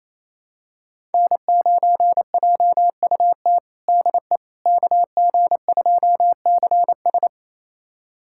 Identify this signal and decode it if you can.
Morse code, heard as a keyed tone, about 28 wpm, 700 Hz. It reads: N9JUT DE KG2CS
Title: USA Callsign DE Callsign - 5 Characters - 28wpm